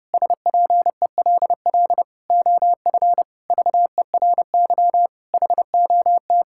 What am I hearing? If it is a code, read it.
Morse code, heard as a keyed tone, about 30 wpm, 700 Hz. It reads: SPELL OF VERY HOT
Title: Sentences from Top 1000 Words - Review Part 5 - 30wpm